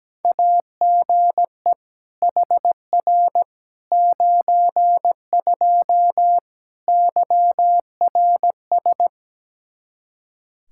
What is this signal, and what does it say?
Morse code, heard as a keyed tone, about 17 wpm, 700 Hz. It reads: AGE HR 92 YRS
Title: Practice QSO - Exchange 3 - More Info - 17wpm